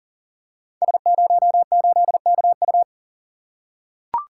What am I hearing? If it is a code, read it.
Morse code, heard as a keyed tone, about 37 wpm, 700 Hz. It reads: S08KU
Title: Code Groups - 5 Letter:Numbers - 40wpm